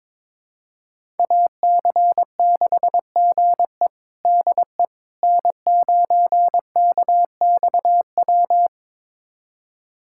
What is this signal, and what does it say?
Morse code, heard as a keyed tone, about 21 wpm, 700 Hz. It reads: AC6GE DE N9KXW
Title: USA Callsign DE Callsign - 5 Characters - 22wpm